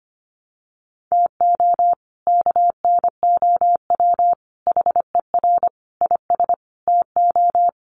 Morse code, heard as a keyed tone, about 25 wpm, 700 Hz. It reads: TO KNOW HER IS TO
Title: Sentences from Top 400 Words 25wpm